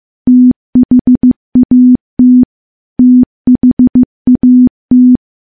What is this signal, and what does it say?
Morse code, heard as a keyed tone, about 15 wpm, 250 Hz. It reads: THAT THAT
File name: morse-training-15WPM